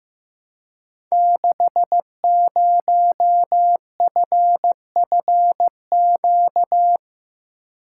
Morse code, heard as a keyed tone, about 15 wpm, 700 Hz. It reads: 60FFQ
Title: Code Groups - 5 Letter:Numbers - 15wpm